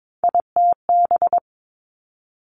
Morse code, heard as a keyed tone, about 22 wpm, 700 Hz. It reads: ITB
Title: Three Challenging Letter-Number Combinations 22wpm